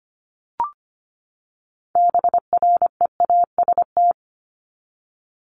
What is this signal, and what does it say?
Morse code, heard as a keyed tone, about 25 wpm, 700 Hz. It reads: BREAST
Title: Top 1751-2000 Words 25wpm